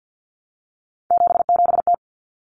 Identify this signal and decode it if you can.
Morse code, heard as a keyed tone, about 48 wpm, 700 Hz. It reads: 77T
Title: Common Cut Numbers - 3-Digit 50wpm